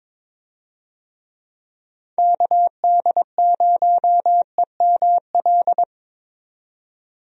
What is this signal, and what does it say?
Morse code, heard as a keyed tone, about 22 wpm, 700 Hz. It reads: KD0EML
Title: USA Call Signs - 6 Characters - 22wpm